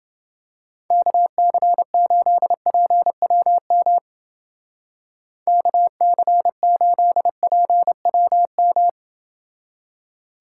Morse code, heard as a keyed tone, about 29 wpm, 700 Hz. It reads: KC8PWM KC8PWM